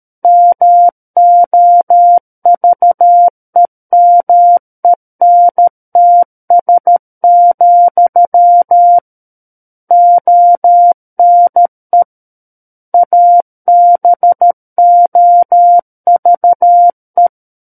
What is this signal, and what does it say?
Morse code, heard as a keyed tone, about 13 wpm, 700 Hz. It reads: MOVEMENTS, ONE ABOVE
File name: Book One / 04-Chapter 4 - 13 WPM